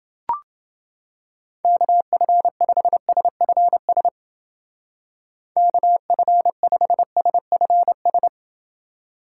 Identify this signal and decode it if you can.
Morse code, heard as a keyed tone, about 28 wpm, 700 Hz. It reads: KF5SFS KF5SFS